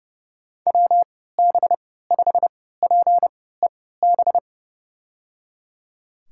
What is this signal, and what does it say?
Morse code, heard as a keyed tone, about 30 wpm, spaced as for 17 wpm, 700 Hz. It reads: WB5PEB
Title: USA Call Signs - 6 Characters - 3x Character Spacing - 30wpm